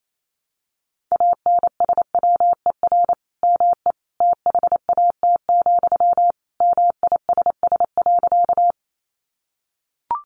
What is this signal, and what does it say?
Morse code, heard as a keyed tone, about 28 wpm, 700 Hz. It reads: ANSWER ME THAT, MISS.
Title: Sentences from Top 500 Words 28wpm